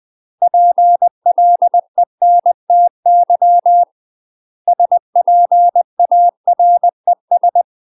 Morse code, heard as a keyed tone, about 20 wpm, 700 Hz. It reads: PLENTY SPARES